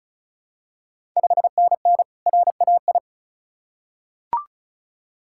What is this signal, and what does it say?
Morse code, heard as a keyed tone, about 36 wpm, 690 Hz. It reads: 5NN RAI
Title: Ontario QSO Party Exchange 35wpm